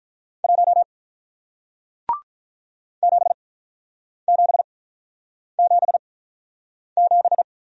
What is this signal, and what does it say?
Morse code, heard as a keyed tone, about 45 wpm, spaced as for 9 wpm, 700 Hz. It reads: 1 7777